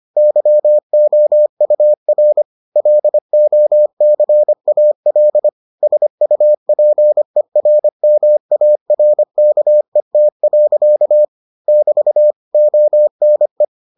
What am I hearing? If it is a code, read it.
Morse code, heard as a keyed tone, about 25 wpm, 600 Hz. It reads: YOUR LOCAL SUPERMARKET. <BT> ONE